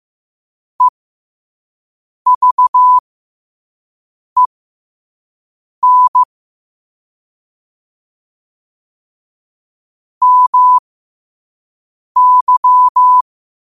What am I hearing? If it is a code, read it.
Morse code, heard as a keyed tone, about 15 wpm, spaced as for 5 wpm, 1000 Hz. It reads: EVEN MY
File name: qotdfiles / 5wpm